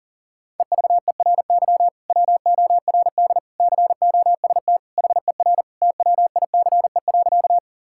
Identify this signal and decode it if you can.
Morse code, heard as a keyed tone, about 40 wpm, 700 Hz. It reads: EVERY WORD COST HER TWICE.
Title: Sentences from Top 1600 Words 40wpm